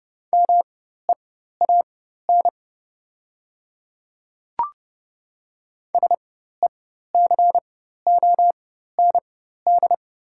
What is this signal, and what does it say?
Morse code, heard as a keyed tone, about 30 wpm, spaced as for 14 wpm, 700 Hz. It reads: MEAN SECOND
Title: 4x Character Spacing - Top 500 Words - 30wpm